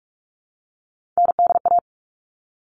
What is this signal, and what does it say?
Morse code, heard as a keyed tone, about 44 wpm, 700 Hz. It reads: NDA
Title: All Cut Numbers - 3-Digit 45wpm